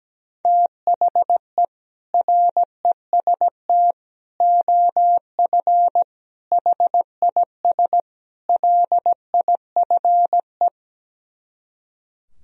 Morse code, heard as a keyed tone, about 17 wpm, 700 Hz. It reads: THE REST OF HIS LIFE
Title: Speed-Racing - Sets of 5 Words 17wpm